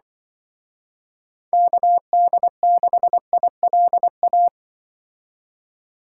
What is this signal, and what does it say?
Morse code, heard as a keyed tone, about 24 wpm, 700 Hz. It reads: KD6ILA
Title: Speed Racing - USA 2x3 Call Signs 20wpm